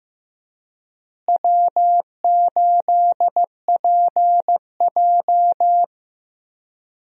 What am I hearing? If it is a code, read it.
Morse code, heard as a keyed tone, about 15 wpm, 700 Hz. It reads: W8PJ
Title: USA Call Signs - 4 Characters - Repeated 3x - 15wpm